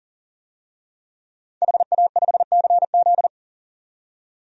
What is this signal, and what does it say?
Morse code, heard as a keyed tone, about 40 wpm, 700 Hz. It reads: HA5CZ
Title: DXCC-100 Call Signs - 5 Characters - 40wpm